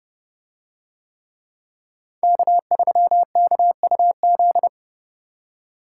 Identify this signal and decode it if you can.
Morse code, heard as a keyed tone, about 29 wpm, 700 Hz. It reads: K3KUZ